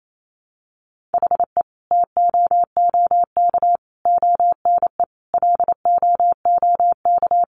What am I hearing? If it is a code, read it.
Morse code, heard as a keyed tone, about 28 wpm, 700 Hz. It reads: HE TOOK ONE LOOK
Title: Sentences from Top 600 Words 28wpm